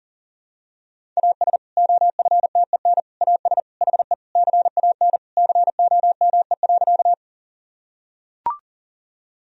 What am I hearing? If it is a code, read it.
Morse code, heard as a keyed tone, about 40 wpm, 700 Hz. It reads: AS OFTEN AS HE CAN COME.
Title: Sentences from Top 1000 Words - Review Part 1 - 40wpm